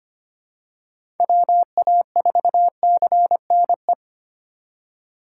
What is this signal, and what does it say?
Morse code, heard as a keyed tone, about 25 wpm, 700 Hz. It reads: WA4CNE
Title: USA Call Signs - 3x Character Spacing - 25wpm